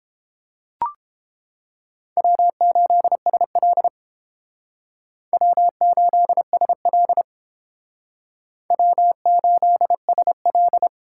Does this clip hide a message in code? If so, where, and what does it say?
Yes: Morse code from 2.2 to 10.9 s, heard as a keyed tone, about 30 wpm, 700 Hz. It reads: W8SL W8SL W8SL